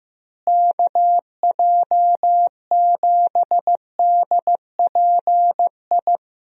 Morse code, heard as a keyed tone, about 15 wpm, 700 Hz. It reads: KJ7DPI
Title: USA Call Signs - 6 Characters - Repeated 3x - 15wpm